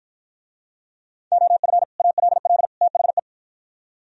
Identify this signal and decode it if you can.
Morse code, heard as a keyed tone, about 52 wpm, 700 Hz. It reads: OF ALL THE